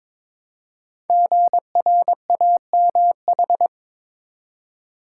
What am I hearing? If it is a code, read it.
Morse code, heard as a keyed tone, about 22 wpm, 700 Hz. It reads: GRAMH
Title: Code Groups - 5 Letters - 22wpm